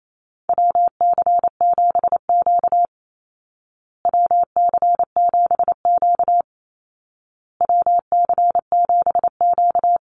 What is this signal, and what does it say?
Morse code, heard as a keyed tone, about 28 wpm, 700 Hz. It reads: WC7Q WC7Q WC7Q
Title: Call Signs Repeated 2x - 28wpm